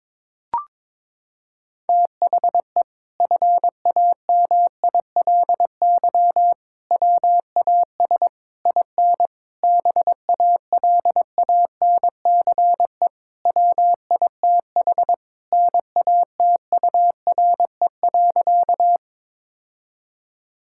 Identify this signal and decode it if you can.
Morse code, heard as a keyed tone, about 22 wpm, 700 Hz. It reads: THE FAMILY WAS IN BALANCE WITH NATURE.
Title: Sentences from Top 1000 Words - Review Part 6 - 22wpm